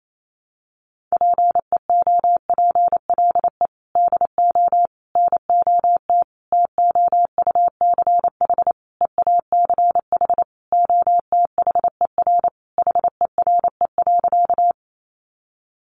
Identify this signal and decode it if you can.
Morse code, heard as a keyed tone, about 28 wpm, 700 Hz. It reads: PEOPLE DO NOT TOUCH EACH OTHER HERE.
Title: Sentences from Top 1200 Words 28wpm